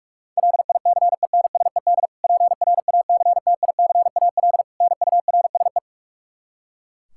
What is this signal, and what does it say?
Morse code, heard as a keyed tone, about 41 wpm, 700 Hz. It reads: LICENSED PRACTICAL NURSE
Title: Practice QSO - Exchange 3 - More Info - 45wpm